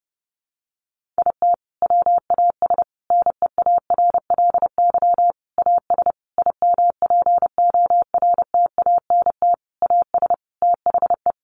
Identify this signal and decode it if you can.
Morse code, heard as a keyed tone, about 28 wpm, 700 Hz. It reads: IT WAS NEARLY AS IMPORTANT AS THE
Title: Sentences from Top 1400 Words 30wpm